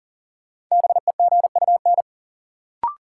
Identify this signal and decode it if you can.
Morse code, heard as a keyed tone, about 40 wpm, 700 Hz. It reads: BEGUN
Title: Sentences from Top 1000 Words - Review Part 6 - New Word Drill - 40wpm